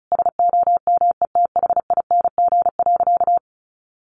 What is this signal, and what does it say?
Morse code, heard as a keyed tone, about 34 wpm, 700 Hz. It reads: SOMETHING.
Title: Sentences from Top 600 Words 35wpm